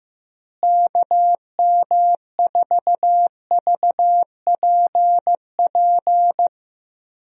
Morse code, heard as a keyed tone, about 14 wpm, 700 Hz. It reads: KM4VPP
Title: USA Call Signs - 6 Characters - Repeated 3x - 15wpm